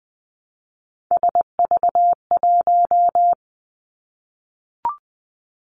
Morse code, heard as a keyed tone, about 20 wpm, 700 Hz. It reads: SV1
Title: Callsign Prefixes for Top 50 DXCC - 20wpm